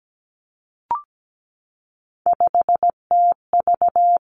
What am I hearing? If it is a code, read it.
Morse code, heard as a keyed tone, about 17 wpm, 700 Hz. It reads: 5TV
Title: Three Challenging Letter-Number Combinations 17wpm